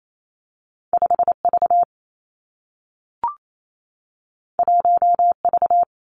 Morse code, heard as a keyed tone, about 28 wpm, 700 Hz. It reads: 5 V 1 V
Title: Two Challenging Letter-Number Combinations 28wpm